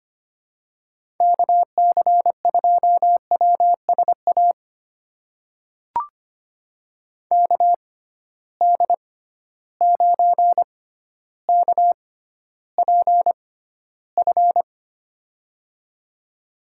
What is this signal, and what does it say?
Morse code, heard as a keyed tone, about 25 wpm, spaced as for 13 wpm, 700 Hz. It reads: KC2WSA KD9KPF
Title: USA Call Signs - 6 Characters - 6x Character Spacing - 25wpm